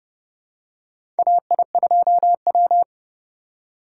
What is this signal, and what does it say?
Morse code, heard as a keyed tone, about 30 wpm, 700 Hz. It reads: AI2W
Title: USA Call Signs - 4 Characters - 3x Character Spacing - 30wpm